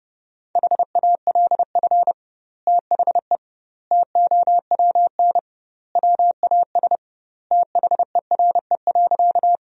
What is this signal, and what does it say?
Morse code, heard as a keyed tone, about 28 wpm, 700 Hz. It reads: HALF THE TOWN WAS THERE.
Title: 2x Word-Spacing - Sentences from Top 500 Words - 30wpm